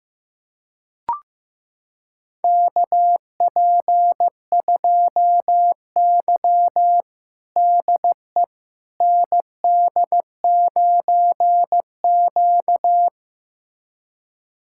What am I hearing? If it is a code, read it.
Morse code, heard as a keyed tone, about 15 wpm, 700 Hz. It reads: KP2Y DE ND9Q
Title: USA Callsign DE Callsign - 4 Characters - 15wpm